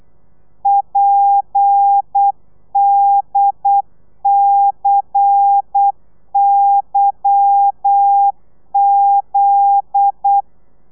Morse code, heard as a keyed tone, about 8 wpm, 800 Hz. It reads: PDCYZ
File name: telegrafia / i2tfj / Lez10 35